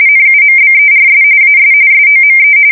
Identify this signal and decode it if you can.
RTTY, about 45.45 baud. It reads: NUMBERS AS SOME
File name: fsk170 / usnavy5